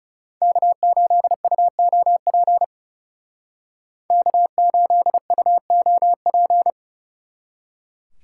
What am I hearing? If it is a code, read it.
Morse code, heard as a keyed tone, about 35 wpm, 700 Hz. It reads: K8UOP K8UOP
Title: Speed Racing - USA 1x3 Call Signs 30wpm